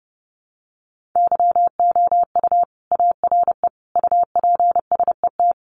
Morse code, heard as a keyed tone, about 30 wpm, 700 Hz. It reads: YOU ARE UPSET
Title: Sentences from Top 800 Words 30wpm